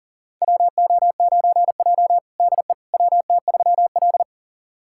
Morse code, heard as a keyed tone, about 40 wpm, 700 Hz. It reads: WO9J DE WT3L
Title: USA Callsign DE Callsign - 4 Characters - 40wpm